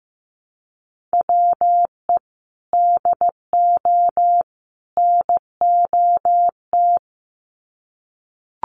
Morse code, heard as a keyed tone, about 15 wpm, 700 Hz. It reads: WE DO NOT
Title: Sets of 3 Words 15wpm